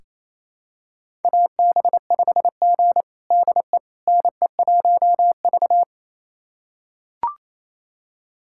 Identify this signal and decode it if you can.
Morse code, heard as a keyed tone, about 28 wpm, 700 Hz. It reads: AB5G DE NE1V
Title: USA Callsign DE Callsign - 4 Characters - 28wpm